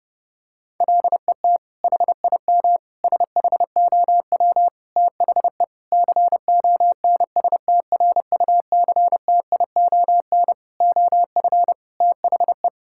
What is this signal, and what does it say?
Morse code, heard as a keyed tone, about 30 wpm, 700 Hz. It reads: LET HIM SHOW THE CONSTRUCTION OF THE
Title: Sentences from Top 1600 Words 30wpm